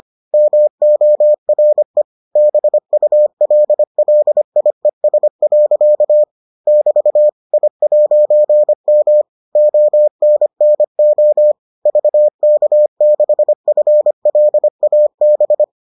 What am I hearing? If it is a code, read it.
Morse code, heard as a keyed tone, about 25 wpm, 600 Hz. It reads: MORE BULLIES. <BT> I'M ONNO VK6FLAB